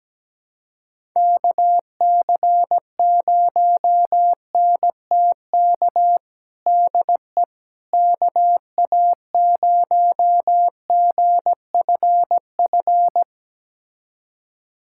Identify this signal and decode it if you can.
Morse code, heard as a keyed tone, about 17 wpm, 700 Hz. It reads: KC0NTK DE KA0GFF
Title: USA Callsign DE Callsign 17wpm